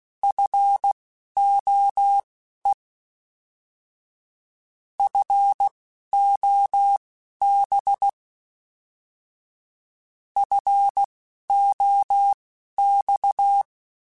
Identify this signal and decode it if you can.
Morse code, heard as a keyed tone, about 16 wpm, spaced as for 11 wpm, 800 Hz. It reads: FOE FOB FOX
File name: Alphabetical / Alphabetical mp3s / Fs @10